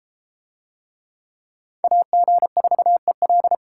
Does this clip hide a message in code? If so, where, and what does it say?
Yes: Morse code from 1.8 to 3.5 s, heard as a keyed tone, about 29 wpm, 700 Hz. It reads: AG4EL